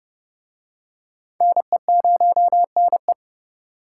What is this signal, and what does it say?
Morse code, heard as a keyed tone, about 29 wpm, 700 Hz. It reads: NE0NE